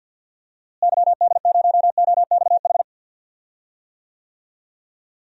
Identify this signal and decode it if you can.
Morse code, heard as a keyed tone, about 52 wpm, 700 Hz. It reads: YD0OXH
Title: DXCC-100 Call Signs - 6 Characters - Repeated 2x - 50wpm